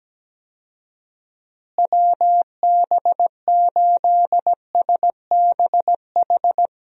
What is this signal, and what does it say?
Morse code, heard as a keyed tone, about 17 wpm, 700 Hz. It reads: WB8SBH
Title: USA Call Signs - 6 Characters - Repeated 3x - 17wpm